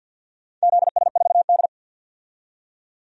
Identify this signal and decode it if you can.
Morse code, heard as a keyed tone, about 51 wpm, 700 Hz. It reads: ZS4D
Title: DXCC-100 Call Signs - 4 Characters - 50wpm